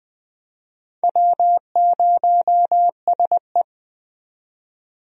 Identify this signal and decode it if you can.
Morse code, heard as a keyed tone, about 20 wpm, 700 Hz. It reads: W0SE
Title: USA Call Signs - 4 Characters - 20wpm